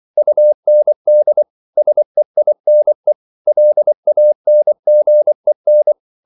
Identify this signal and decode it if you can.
Morse code, heard as a keyed tone, about 24 wpm, 600 Hz. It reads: UND SEINE LANGEN